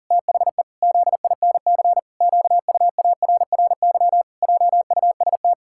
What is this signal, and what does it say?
Morse code, heard as a keyed tone, about 40 wpm, 700 Hz. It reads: THE ZINC QUARRY JUST
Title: Pangram Challenge v2 - 40wpm